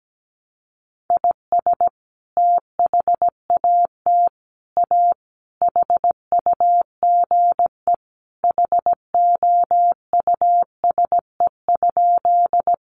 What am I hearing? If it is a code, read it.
Morse code, heard as a keyed tone, about 17 wpm, 700 Hz. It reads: IS THAT A HUGE HOUSE?
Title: Sentences from Top 1100 Words 17wpm